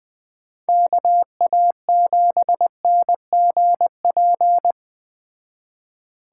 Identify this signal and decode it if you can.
Morse code, heard as a keyed tone, about 20 wpm, 700 Hz. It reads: KA7NGP